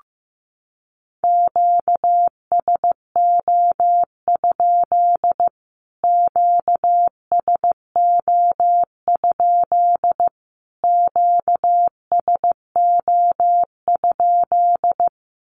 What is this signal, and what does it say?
Morse code, heard as a keyed tone, about 15 wpm, 700 Hz. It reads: QSO? QSO? QSO?
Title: Q-Codes Repeated 3x 15wpm